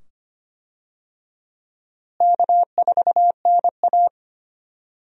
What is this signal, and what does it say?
Morse code, heard as a keyed tone, about 25 wpm, 700 Hz. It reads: K4NA
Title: USA Call Signs - 4 Characters - Repeated 2x - 25wpm